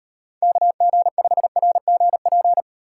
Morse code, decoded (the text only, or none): KG5RGP